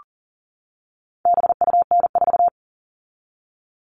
Morse code, heard as a keyed tone, about 40 wpm, 700 Hz. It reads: BUN4
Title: All Cut Numbers - 4-Digit 40wpm